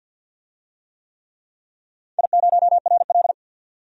Morse code, heard as a keyed tone, about 52 wpm, 700 Hz. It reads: I0RL
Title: DXCC-100 Call Signs - 4 Characters - 50wpm